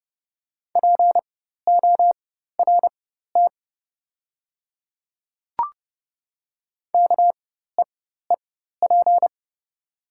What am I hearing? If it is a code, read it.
Morse code, heard as a keyed tone, about 30 wpm, spaced as for 14 wpm, 700 Hz. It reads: PORT KEEP
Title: 4x Character Spacing - Top 300 Words - 30wpm